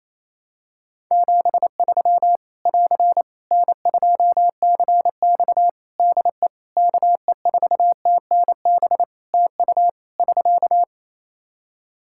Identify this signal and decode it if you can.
Morse code, heard as a keyed tone, about 27 wpm, 700 Hz. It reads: 73 <AR> N2CX DE KE4TNB TU <SK>
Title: Practice QSO - Exchange 4 - Ending - 28wpm